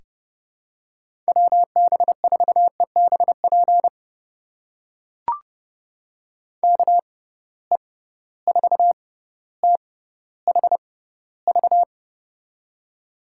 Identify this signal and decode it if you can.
Morse code, heard as a keyed tone, about 31 wpm, spaced as for 15 wpm, 700 Hz. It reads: WB4EBP KE4THV
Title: USA Call Signs - 6 Characters - 6x Character Spacing - 30wpm